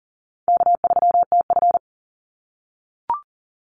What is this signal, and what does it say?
Morse code, heard as a keyed tone, about 40 wpm, 700 Hz. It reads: K3TF
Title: Easy Call Signs - 40wpm